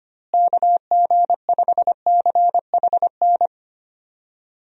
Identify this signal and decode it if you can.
Morse code, heard as a keyed tone, about 25 wpm, 700 Hz. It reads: KG5CHN
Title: USA Call Signs - 6 Characters - Repeated 3x - 25wpm